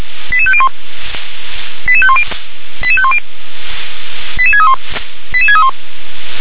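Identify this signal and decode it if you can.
Five-tone selective call (ZVEI), heard as quick runs of five tones, tones of about 70 ms. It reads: 88501 88510 88510 88631 88631
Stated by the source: ZVEI five-tone selective call received on an SDR.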